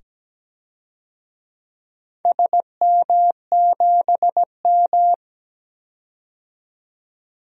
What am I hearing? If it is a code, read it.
Morse code, heard as a keyed tone, about 17 wpm, 700 Hz. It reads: SM7M
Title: DXCC-100 Call Signs - 4 Characters - 17wpm